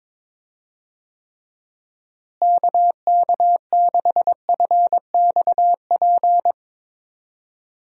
Morse code, heard as a keyed tone, about 22 wpm, 700 Hz. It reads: KK6FXP
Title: USA Call Signs - 6 Characters - Repeated 2x - 22wpm